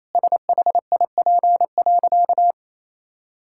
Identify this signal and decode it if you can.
Morse code, heard as a keyed tone, about 28 wpm, 700 Hz. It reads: SHIP.
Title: Sentences from Top 1000 Words - Review Part 5 - 28wpm